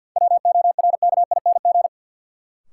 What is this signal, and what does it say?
Morse code, heard as a keyed tone, about 54 wpm, 700 Hz. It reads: WORKING
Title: Sentences from Top 700 Words - New Word Drill - 50wpm